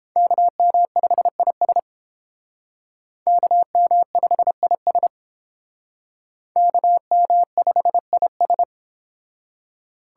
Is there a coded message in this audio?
Yes: Morse code from 0.2 to 8.6 s, heard as a keyed tone, about 31 wpm, 700 Hz. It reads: KM5IS KM5IS KM5IS